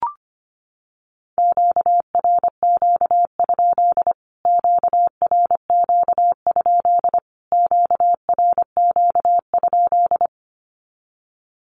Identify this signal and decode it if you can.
Morse code, heard as a keyed tone, about 25 wpm, 700 Hz. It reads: QRQ? QRQ? QRQ?
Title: Q-Codes Repeated 3x 25wpm